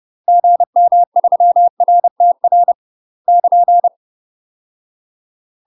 Morse code, heard as a keyed tone, about 30 wpm, 700 Hz. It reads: GM3RTR <KN>
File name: UK MASTER QSOs TRAINING / 30WPM / UK MASTER QSOs 0009